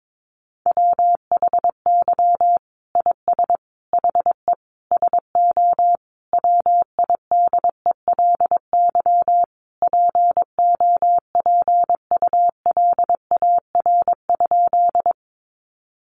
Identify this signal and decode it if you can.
Morse code, heard as a keyed tone, about 22 wpm, 700 Hz. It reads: WHY IS HE SO WIDELY POPULAR?
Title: Sentences from Top 1200 Words 22wpm